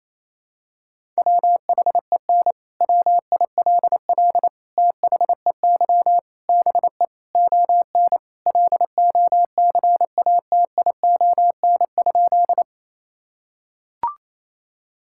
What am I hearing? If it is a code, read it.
Morse code, heard as a keyed tone, about 28 wpm, 700 Hz. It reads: WHEN WILL THEY BE ON LOCATION?
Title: Sentences from Top 1000 Words - Review Part 6 - 28wpm